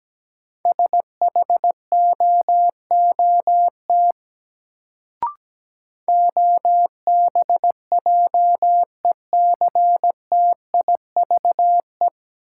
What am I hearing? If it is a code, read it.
Morse code, heard as a keyed tone, about 17 wpm, 700 Hz. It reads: SHOOT OBJECTIVE
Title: Sentences from Top 1200 Words - New Word Drill Alt - 17wpm